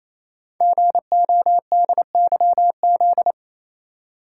Morse code, heard as a keyed tone, about 28 wpm, 700 Hz. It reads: GODYZ
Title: Code Groups - 5 Letters - 28wpm